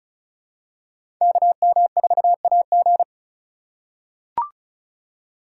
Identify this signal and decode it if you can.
Morse code, heard as a keyed tone, about 34 wpm, 700 Hz. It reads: KM4AG
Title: USA Call Signs - 4x Character Spacing - 35wpm